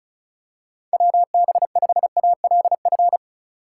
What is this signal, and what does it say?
Morse code, heard as a keyed tone, about 36 wpm, 700 Hz. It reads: WB5ALF
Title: USA Call Signs - 6 Characters - 3x Character Spacing - 35wpm